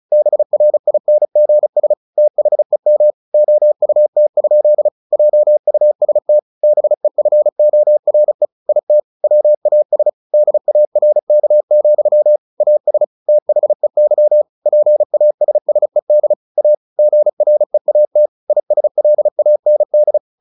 Morse code, heard as a keyed tone, about 34 wpm, 600 Hz. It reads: BRINGS THEM OUT? JUST BEFORE IT WAS DARK, AS THEY PASSED A GREAT ISLAND
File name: the old man and the sea 2 35wpm